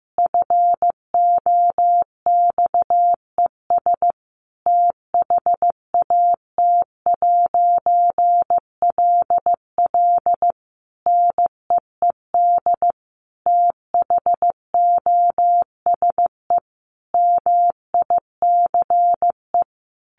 Morse code, heard as a keyed tone, about 15 wpm, 700 Hz. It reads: FOXES THAT'LL NEED THOSE MICE
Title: A Sound Of Thunder 15wpm